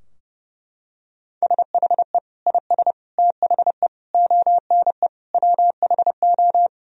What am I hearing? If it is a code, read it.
Morse code, heard as a keyed tone, about 32 wpm, 700 Hz. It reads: SHE IS THE ONE WHO
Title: Sentences from Top 1000 Words - Review Part 2 - 30wpm